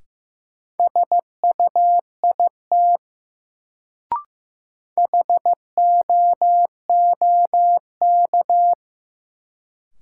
Morse code, heard as a keyed tone, about 15 wpm, 700 Hz. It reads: SUIT HOOK
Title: Sentences from Top 1600 Words - New Word Drill Alt - 15wpm